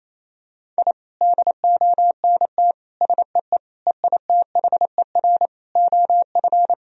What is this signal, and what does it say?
Morse code, heard as a keyed tone, about 27 wpm, 700 Hz. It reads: I DONT SEE EITHER OF
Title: Sentences from Top 1000 Words - Review Part 4 - 28wpm